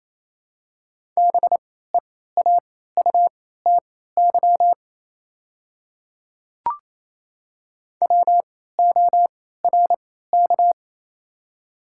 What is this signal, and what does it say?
Morse code, heard as a keyed tone, about 28 wpm, spaced as for 16 wpm, 700 Hz. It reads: BEAUTY WORK
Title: 3x Character Spacing - Top 500 Words - 28wpm